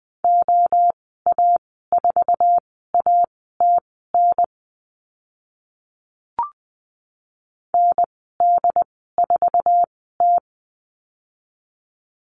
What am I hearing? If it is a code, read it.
Morse code, heard as a keyed tone, about 20 wpm, spaced as for 14 wpm, 700 Hz. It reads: OA4ATN ND4T
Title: 076 Lesson 13g - Letter L - Call Signs 20wpm